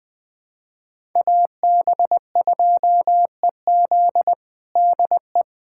Morse code, heard as a keyed tone, about 20 wpm, 700 Hz. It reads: AB2EZ DE
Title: USA Callsign DE Callsign - 5 Characters - 20wpm